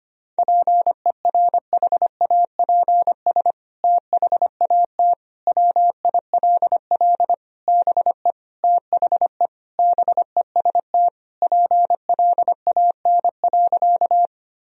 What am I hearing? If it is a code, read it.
Morse code, heard as a keyed tone, about 25 wpm, 700 Hz. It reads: PERHAPS THAT WILL BE THE BEST PLAN.
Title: Sentences from Top 1000 Words - Review Part 2 - 25wpm